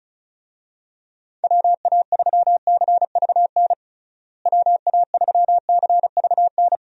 Morse code, heard as a keyed tone, about 34 wpm, 700 Hz. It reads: WA3CVN WA3CVN
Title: USA Call Signs - 6 Characters - Repeated 3x - 35wpm